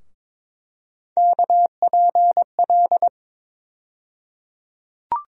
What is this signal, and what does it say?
Morse code, heard as a keyed tone, about 21 wpm, 700 Hz. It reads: KPL
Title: USA Call Sign Postfixes - 3 Letters 22wpm